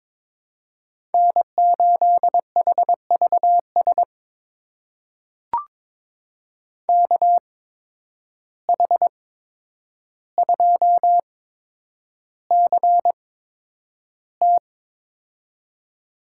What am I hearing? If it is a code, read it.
Morse code, heard as a keyed tone, about 22 wpm, spaced as for 9 wpm, 700 Hz. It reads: N8HVS KH2CT
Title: USA Call Signs - 5 Characters - 8x Character Spacing - 22wpm